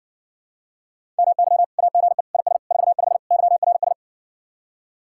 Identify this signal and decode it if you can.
Morse code, heard as a keyed tone, about 60 wpm, 700 Hz. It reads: MY AGE IS 45 YRS